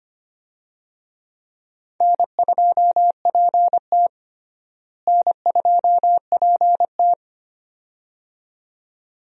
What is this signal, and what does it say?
Morse code, heard as a keyed tone, about 25 wpm, 700 Hz. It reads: N2PT N2PT
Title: USA Call Signs - 4 Characters - Repeated 2x - 25wpm